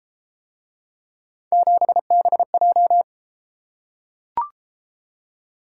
Morse code, heard as a keyed tone, about 33 wpm, 700 Hz. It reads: 7BJ